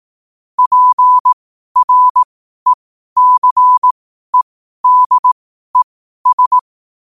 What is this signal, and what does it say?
Morse code, heard as a keyed tone, about 18 wpm, spaced as for 12 wpm, 1000 Hz. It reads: PRECEDES
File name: qotdfiles / 12wpm